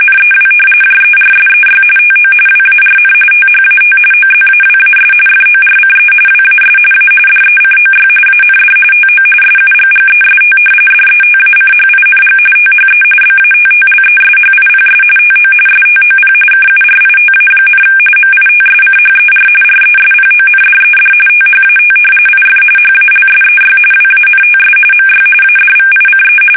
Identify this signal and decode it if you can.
RTTY, about 45.45 baud. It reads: NEW JERSEY TO WARN THAT OZAWAS FORCE WAS A DECOY AND TO SUGGEST DETACHING THE BATTLESHIPS TO STOP KURITA, MITSCHER RESPONDED THAT IF HALSEY WANTS MY ADVICE
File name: fsk850 / battleshipnj